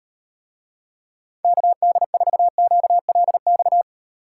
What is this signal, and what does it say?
Morse code, heard as a keyed tone, about 38 wpm, 700 Hz. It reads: KD4QLX